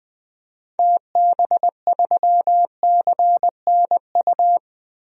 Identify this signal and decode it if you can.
Morse code, heard as a keyed tone, about 20 wpm, 700 Hz. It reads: TB3CNU
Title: DXCC-100 Call Signs - 6 Characters - 20wpm